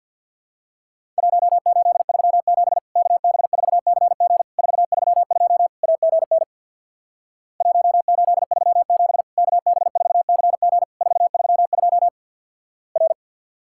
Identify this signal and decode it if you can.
Morse code, heard as a keyed tone, about 50 wpm, 700 Hz. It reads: 1837 KB4CG 432 AGN 1837 KB4CG 432 R
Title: Worked All Europe DX Contest - QTC Exchange with Repeats - 50wpm